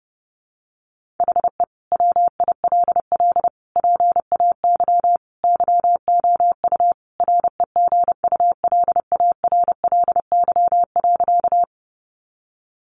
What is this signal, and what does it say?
Morse code, heard as a keyed tone, about 30 wpm, 700 Hz. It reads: HE WILL PAY YOU REGULARLY.
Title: Sentences from Top 1400 Words 30wpm